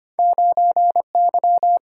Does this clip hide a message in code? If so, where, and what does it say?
Yes: Morse code from 0.2 to 1.8 s, heard as a keyed tone, about 33 wpm, 700 Hz. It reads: 9Y